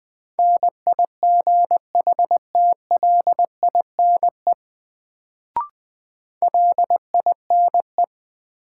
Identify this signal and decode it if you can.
Morse code, heard as a keyed tone, about 20 wpm, 700 Hz. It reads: NIGHTLINE LINE
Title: Sets of 5 Words - New Word Drill - 20wpm